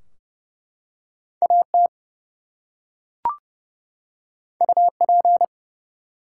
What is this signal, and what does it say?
Morse code, heard as a keyed tone, about 30 wpm, 700 Hz. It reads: AT UP
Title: Two Letter Words - 30wpm